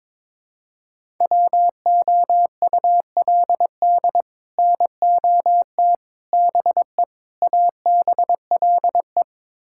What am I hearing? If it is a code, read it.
Morse code, heard as a keyed tone, about 22 wpm, 700 Hz. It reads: WOULD NOT BE ABLE
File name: practice / 22wpm / Build A Phrase - 5-Word Phrases - 22wpm